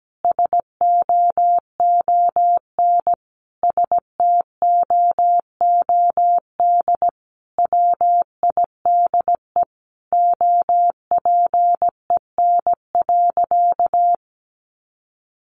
Morse code, heard as a keyed tone, about 17 wpm, 700 Hz. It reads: SOON STOOD WIDE OPEN.
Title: Sentences from Top 900 Words 17wpm